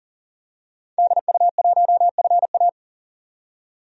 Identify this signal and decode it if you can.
Morse code, heard as a keyed tone, about 40 wpm, 700 Hz. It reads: DU1FA
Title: DXCC-100 Call Signs - 5 Characters - 40wpm